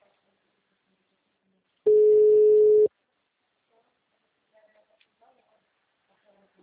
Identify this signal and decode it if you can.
Ringback tone (Europe, 425 Hz), with speech.